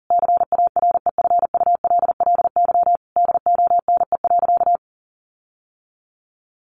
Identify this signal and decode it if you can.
Morse code, heard as a keyed tone, about 40 wpm, 700 Hz. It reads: CAREFULLY DONE.
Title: Sentences from Top 1300 Words 40wpm